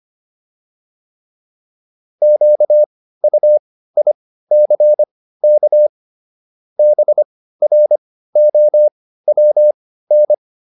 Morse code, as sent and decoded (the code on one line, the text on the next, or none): --.- ..- .. -.-. -.- / -... .-. --- .-- -.
QUICK BROWN